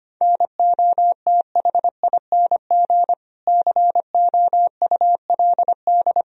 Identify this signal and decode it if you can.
Morse code, heard as a keyed tone, about 25 wpm, 700 Hz. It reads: NOTHING COULD
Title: Sentences from Top 500 Words and Abbreviations 25wpm